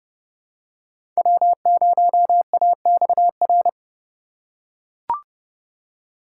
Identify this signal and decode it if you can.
Morse code, heard as a keyed tone, about 30 wpm, 700 Hz. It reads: W0AXR
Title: USA Call Signs - 5 Characters - 2x Character Spacing - 30wpm